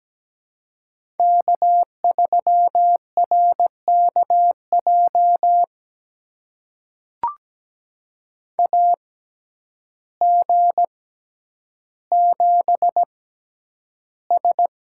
Morse code, heard as a keyed tone, about 17 wpm, spaced as for 9 wpm, 700 Hz. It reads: K3RKJ AG7S
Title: USA Call Signs - 5 Characters - 6x Character Spacing - 17wpm